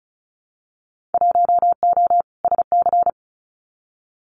Morse code, heard as a keyed tone, about 36 wpm, 700 Hz. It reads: 1O SC
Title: Winter Field Day Exchanges 35wpm